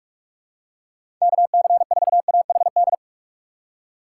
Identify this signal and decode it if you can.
Morse code, heard as a keyed tone, about 41 wpm, 700 Hz. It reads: KC4AHD
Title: USA Call Signs - 6 Characters - 4x Character Spacing - 45wpm